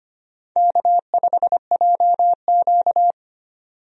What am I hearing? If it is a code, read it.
Morse code, heard as a keyed tone, about 25 wpm, 700 Hz. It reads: K5JQ
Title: USA Call Signs - 4 Characters - 25wpm